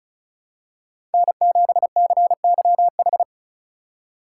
Morse code, heard as a keyed tone, about 36 wpm, 700 Hz. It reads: N7CYH
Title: USA Call Signs - 5 Characters - 2x Character Spacing - 35wpm